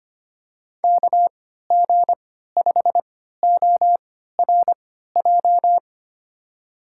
Morse code, heard as a keyed tone, about 25 wpm, spaced as for 14 wpm, 700 Hz. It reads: KG5ORJ
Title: USA Call Signs - 3x Character Spacing - 25wpm